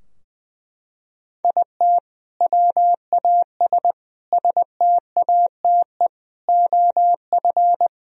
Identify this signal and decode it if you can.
Morse code, heard as a keyed tone, about 20 wpm, 700 Hz. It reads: IT WAS STATE OF
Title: Sentences from Top 1000 Words - Review Part 3 - 20wpm